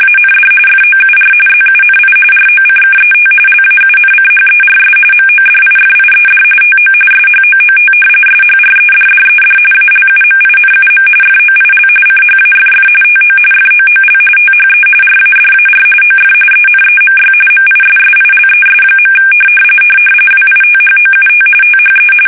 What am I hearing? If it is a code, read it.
RTTY, about 45.45 baud. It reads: AND FOUR ZUNI ROCKET LAUNCHERS MODIFIED TO SHOOT SHREDDED ALUMINUM FOIL CHAFF TO FOOL THE RADAR OF ENEMY MISSILES. WITHOUT THE
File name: fsk850 / battleshipnj